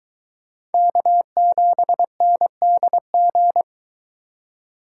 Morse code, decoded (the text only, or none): K7NDG